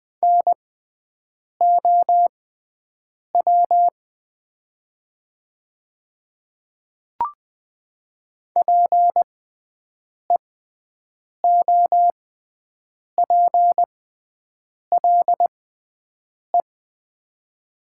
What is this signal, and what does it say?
Morse code, heard as a keyed tone, about 20 wpm, spaced as for 7 wpm, 700 Hz. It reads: NOW PEOPLE
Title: 6x Character Spacing - Top 300 Words - 20wpm